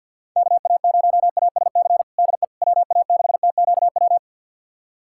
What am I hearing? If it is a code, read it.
Morse code, heard as a keyed tone, about 52 wpm, 690 Hz. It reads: KA0ASC DE WA6TQW
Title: USA Callsign DE Callsign - 6 Characters - 50wpm